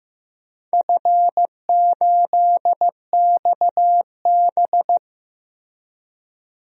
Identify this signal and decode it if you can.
Morse code, heard as a keyed tone, about 15 wpm, 700 Hz. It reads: F8XB
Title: DXCC-100 Call Signs - 4 Characters - Repeated 3x - 15wpm